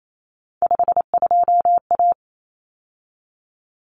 Morse code, heard as a keyed tone, about 28 wpm, 700 Hz. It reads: 52A
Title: Common Cut Numbers - 3-Digit 28wpm